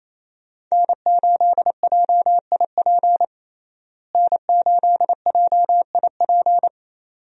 Morse code, heard as a keyed tone, about 28 wpm, 700 Hz. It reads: N8JIP N8JIP
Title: USA Call Signs - 5 Characters - Repeated 2x - 28wpm